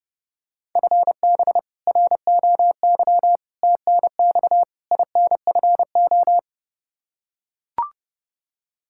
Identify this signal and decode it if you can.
Morse code, heard as a keyed tone, about 30 wpm, 700 Hz. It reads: FB ROY TNX INFO
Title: K1USN SST QSO Practice 30wpm